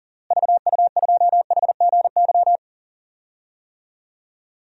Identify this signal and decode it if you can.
Morse code, heard as a keyed tone, about 38 wpm, 700 Hz. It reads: VU2HGY
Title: DXCC-100 Call Signs - 6 Characters - Repeated 3x - 40wpm